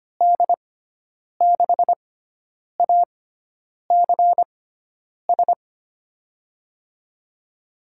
Morse code, heard as a keyed tone, about 25 wpm, spaced as for 9 wpm, 700 Hz. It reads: D6ACS
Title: USA Call Signs - 6x Character Spacing - 25wpm